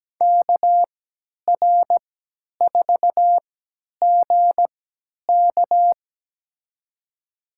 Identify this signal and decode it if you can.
Morse code, heard as a keyed tone, about 17 wpm, spaced as for 10 wpm, 700 Hz. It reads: KR4GK
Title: USA Call Signs - 5 Characters - 3x Character Spacing - 17wpm